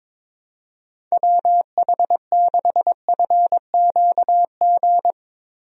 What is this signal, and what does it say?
Morse code, heard as a keyed tone, about 21 wpm, 700 Hz. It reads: WH6FQG
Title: USA Call Signs - 6 Characters - Repeated 3x - 22wpm